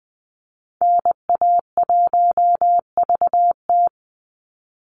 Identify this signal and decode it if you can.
Morse code, heard as a keyed tone, about 20 wpm, 700 Hz. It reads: NA1VT
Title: Call Signs Repeated 3x - 20wpm